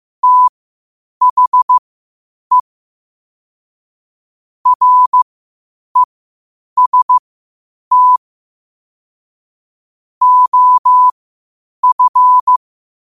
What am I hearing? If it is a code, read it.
Morse code, heard as a keyed tone, about 15 wpm, spaced as for 8 wpm, 1000 Hz. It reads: THE REST OF
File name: qotdfiles / 8wpm